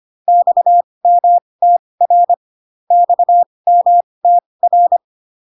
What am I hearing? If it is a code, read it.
Morse code, heard as a keyed tone, about 25 wpm, spaced as for 20 wpm, 700 Hz. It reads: XMTR XMTR